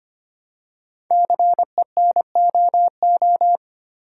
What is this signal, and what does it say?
Morse code, heard as a keyed tone, about 25 wpm, 700 Hz. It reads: CENOO
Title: Code Groups - 5 Letters - 25wpm